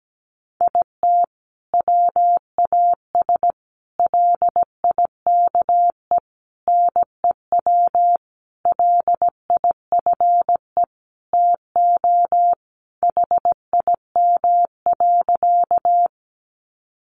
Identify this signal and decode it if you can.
Morse code, heard as a keyed tone, about 17 wpm, 700 Hz. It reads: IT WAS LIKE NEW LIFE TO HIM.
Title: Sentences from Top 300 Words 17wpm